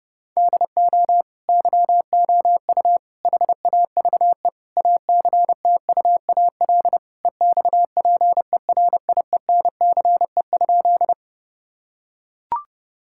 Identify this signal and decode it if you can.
Morse code, heard as a keyed tone, about 30 wpm, 700 Hz. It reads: DO YOU HAVE ACTUAL EXPERIENCE?
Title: Sentences from Top 1000 Words - Review Part 5 - 30wpm